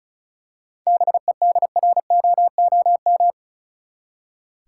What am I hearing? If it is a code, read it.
Morse code, heard as a keyed tone, about 34 wpm, 700 Hz. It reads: BEDROOM
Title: Seven Letter Words - 35wpm